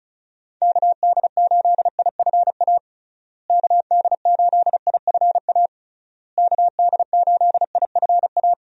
Morse code, heard as a keyed tone, about 36 wpm, 700 Hz. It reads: KD8IFA KD8IFA KD8IFA
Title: USA Call Signs - 6 Characters - Repeated 3x - 35wpm